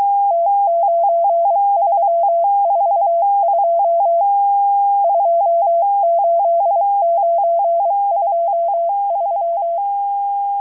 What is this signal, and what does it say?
Morse code, heard as a keyed tone, about 23 wpm, 680 Hz. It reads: T9342 28923
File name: HM02- -0437utc-7351khz